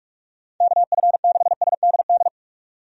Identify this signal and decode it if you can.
Morse code, heard as a keyed tone, about 46 wpm, 700 Hz. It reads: KF6SDD
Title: USA Call Signs - 6 Characters - 45wpm